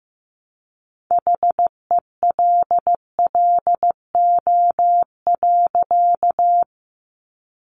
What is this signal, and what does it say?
Morse code, heard as a keyed tone, about 15 wpm, 700 Hz. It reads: HELLO.
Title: Sentences from Top 1100 Words 15wpm